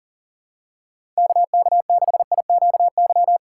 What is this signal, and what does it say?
Morse code, heard as a keyed tone, about 40 wpm, 700 Hz. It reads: KK6IQY
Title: USA Call Signs - 6 Characters - 6x Character Spacing - 40wpm